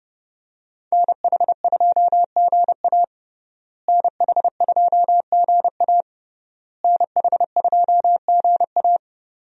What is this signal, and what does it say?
Morse code, heard as a keyed tone, about 30 wpm, 700 Hz. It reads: NH2GA NH2GA NH2GA
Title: USA Call Signs - 5 Characters - Repeated 3x - 30wpm